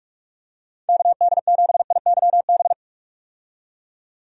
Morse code, heard as a keyed tone, about 46 wpm, 700 Hz. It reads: KD7IYB
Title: USA Call Signs - 6 Characters - 45wpm